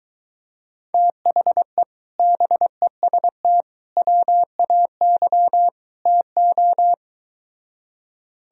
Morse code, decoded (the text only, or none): THE BEST WAY TO